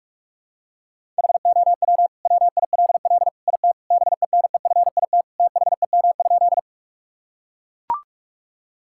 Morse code, heard as a keyed tone, about 42 wpm, 700 Hz. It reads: HOW WILL IT BENEFIT THEM?
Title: Sentences from Top 1000 Words - Review Part 6 - 45wpm